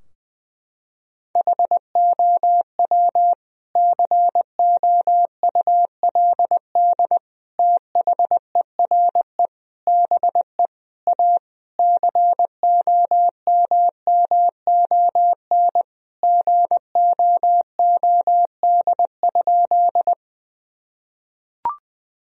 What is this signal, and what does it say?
Morse code, heard as a keyed tone, about 21 wpm, 700 Hz. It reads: HOW COULD THERE BE A COMMON GOOD?
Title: Sentences from Top 1000 Words - Review Part 3 - 20wpm